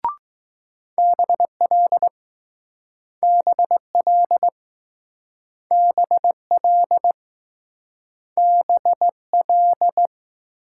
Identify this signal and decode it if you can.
Morse code, heard as a keyed tone, about 20 wpm, 700 Hz. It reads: BL BL BL BL